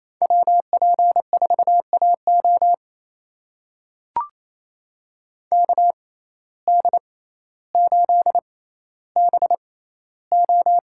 Morse code, heard as a keyed tone, about 28 wpm, spaced as for 14 wpm, 700 Hz. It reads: WP4AO KD8BO
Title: USA Call Signs - 5 Characters - 6x Character Spacing - 28wpm